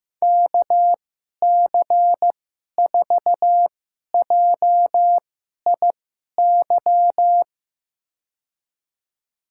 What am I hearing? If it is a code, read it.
Morse code, heard as a keyed tone, about 15 wpm, spaced as for 11 wpm, 700 Hz. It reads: KC4JIY
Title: USA Call Signs - 6 Characters - 2x Character Spacing - 15wpm